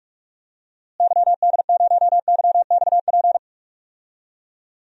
Morse code, heard as a keyed tone, about 46 wpm, 700 Hz. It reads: YD0YXP
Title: DXCC-100 Call Signs - 6 Characters - Repeated 2x - 45wpm